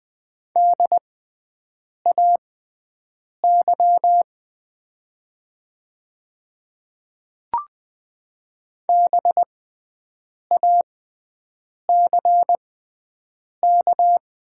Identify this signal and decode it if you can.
Morse code, heard as a keyed tone, about 20 wpm, spaced as for 7 wpm, 700 Hz. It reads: DAY BACK
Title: 6x Character Spacing - Top 200 Words - 20wpm